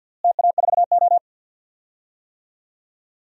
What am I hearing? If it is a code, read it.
Morse code, heard as a keyed tone, about 50 wpm, 700 Hz. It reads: TA4O
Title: DXCC-100 Call Signs - 4 Characters - Repeated 3x - 50wpm